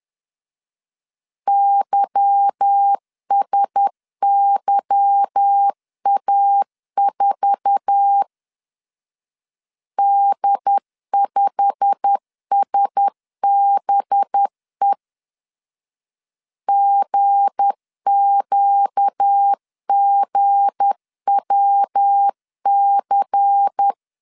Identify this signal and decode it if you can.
Morse code, heard as a keyed tone, about 10 wpm, 800 Hz. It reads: YSYA4 D5SBE GQGWC
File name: CW / DET / 1 9 1500C S B 10wpm 4439